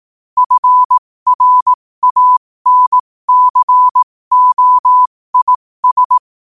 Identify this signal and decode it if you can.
Morse code, heard as a keyed tone, about 18 wpm, spaced as for 15 wpm, 1000 Hz. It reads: FRANCOIS
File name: qotdfiles / 15wpm